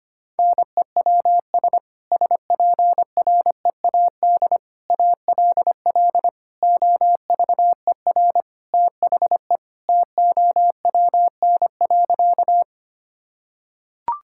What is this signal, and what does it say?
Morse code, heard as a keyed tone, about 25 wpm, 700 Hz. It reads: NEWS SPREAD ALL OVER THE TOWN.
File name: practice / 25wpm / Sentences with 3-Word Phrases - 25wpm